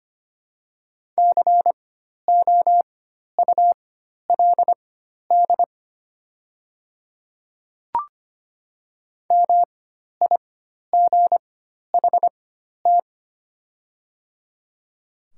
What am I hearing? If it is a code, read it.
Morse code, heard as a keyed tone, about 25 wpm, spaced as for 12 wpm, 700 Hz. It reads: COULD MIGHT
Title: 4x Character Spacing - Top 500 Words - 25wpm